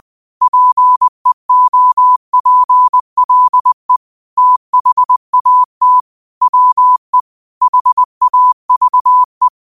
Morse code, heard as a keyed tone, about 20 wpm, 1000 Hz. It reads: PEOPLE THAT WE HAVE
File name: 20wpm